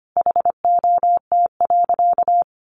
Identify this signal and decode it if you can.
Morse code, heard as a keyed tone, about 25 wpm, 700 Hz. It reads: HOT.
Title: Easy Phrases 25wpm